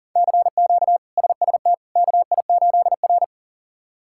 Morse code, heard as a keyed tone, about 40 wpm, 700 Hz. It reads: CQ SST KI8R
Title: K1USN SST QSO Practice - 40wpm